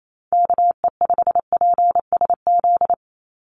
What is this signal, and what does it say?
Morse code, heard as a keyed tone, about 28 wpm, 700 Hz. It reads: KE5PSZ
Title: USA Call Signs 28wpm